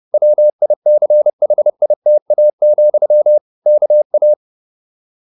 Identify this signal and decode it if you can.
Morse code, heard as a keyed tone, about 30 wpm, 600 Hz. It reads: WICHITA, KA